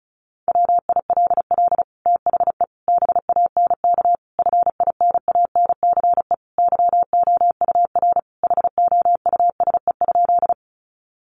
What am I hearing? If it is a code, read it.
Morse code, heard as a keyed tone, about 34 wpm, 700 Hz. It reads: WILL THE BANK FINANCE YOUR HOUSE?
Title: Sentences from Top 1500 Words 35wpm